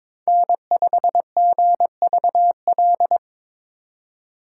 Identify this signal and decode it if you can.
Morse code, heard as a keyed tone, about 26 wpm, 700 Hz. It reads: N5GVL
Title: Speed Racing - USA 1x3 Call Signs 22wpm